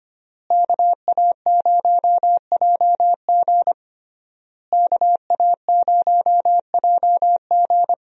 Morse code, heard as a keyed tone, about 25 wpm, 700 Hz. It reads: KA0JG KA0JG
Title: USA Call Signs - 5 Characters - Repeated 3x - 25wpm